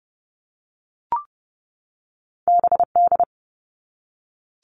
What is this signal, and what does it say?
Morse code, heard as a keyed tone, about 32 wpm, 700 Hz. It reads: BD
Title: All Cut Numbers - 2-Digit 30wpm